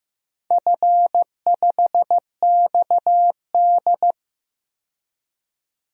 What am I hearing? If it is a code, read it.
Morse code, heard as a keyed tone, about 15 wpm, 700 Hz. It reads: F5XD
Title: DXCC-100 Call Signs - 4 Characters - 15wpm